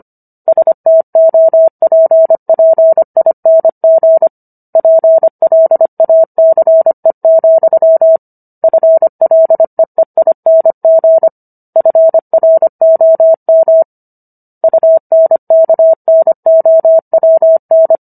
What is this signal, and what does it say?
Morse code, heard as a keyed tone, about 25 wpm, 650 Hz. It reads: STOPPING PLACE, FLEEING FROM UNKNOWN
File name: Book One / 16-Chapter 16 - 25 WPM